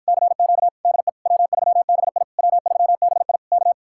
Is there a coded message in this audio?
Yes: Morse code from 0.1 to 3.7 s, heard as a keyed tone, about 54 wpm, 700 Hz. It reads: CQ DE W3BI W3BI K